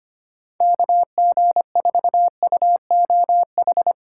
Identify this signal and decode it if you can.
Morse code, heard as a keyed tone, about 25 wpm, 700 Hz. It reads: KG4UOH
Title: USA Callsign DE Callsign - 6 Characters - 25wpm